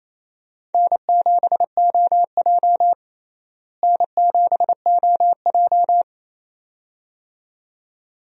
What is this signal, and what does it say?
Morse code, heard as a keyed tone, about 28 wpm, 700 Hz. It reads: N7OJ N7OJ
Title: USA Call Signs - 4 Characters - Repeated 2x - 28wpm